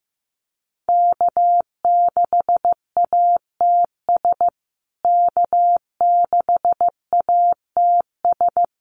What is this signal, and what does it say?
Morse code, heard as a keyed tone, about 15 wpm, 700 Hz. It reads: K6ATS K6ATS
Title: USA Call Signs Repeated 2x 15wpm